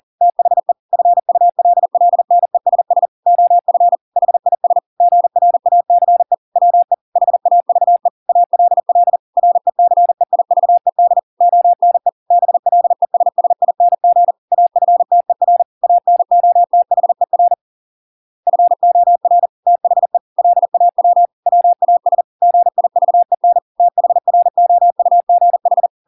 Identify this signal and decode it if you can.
Morse code, heard as a keyed tone, about 40 wpm, 700 Hz. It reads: THE FULLNESS OF HIS GRACE WE HAVE ALL RECEIVED ONE BLESSING AFTER ANOTHER FOR THE LAW WAS GIVEN THROUGH